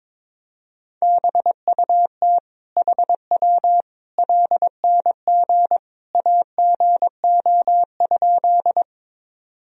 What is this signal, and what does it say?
Morse code, heard as a keyed tone, about 22 wpm, 700 Hz. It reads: BUT HW LNG AGO?
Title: Sentences from Top 500 Words and Abbreviations 22wpm